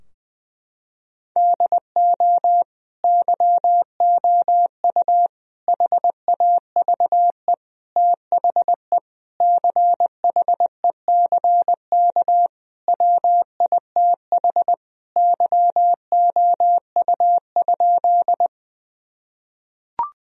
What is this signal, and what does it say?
Morse code, heard as a keyed tone, about 20 wpm, 700 Hz. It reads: DO YOU HAVE THE CHECK WITH YOU?
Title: Sentences from Top 1000 Words - Review Part 3 - 20wpm